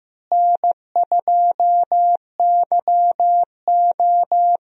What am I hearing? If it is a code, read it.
Morse code, heard as a keyed tone, about 16 wpm, 700 Hz. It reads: N2YO
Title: USA Call Signs - 4 Characters - Repeated 3x - 15wpm